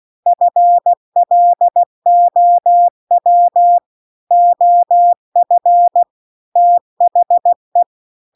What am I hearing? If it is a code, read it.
Morse code, heard as a keyed tone, about 16 wpm, 700 Hz. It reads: FLOW OF THE